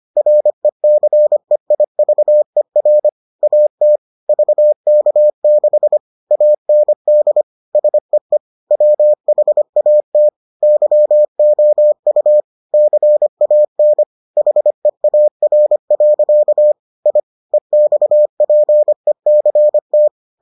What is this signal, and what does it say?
Morse code, heard as a keyed tone, about 25 wpm, 600 Hz. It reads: RECEIVER AT VK6 AND SEE WHAT YOU CAN HEAR. I EXPECT